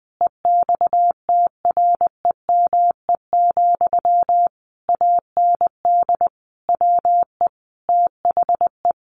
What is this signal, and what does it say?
Morse code, heard as a keyed tone, about 20 wpm, 700 Hz. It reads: EXTREME, AND WE THE
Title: Sentences from Top 1300 Words 20wpm